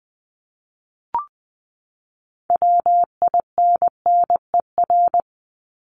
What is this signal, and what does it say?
Morse code, heard as a keyed tone, about 20 wpm, 700 Hz. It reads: WINNER
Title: Top 1751-2000 Words 20wpm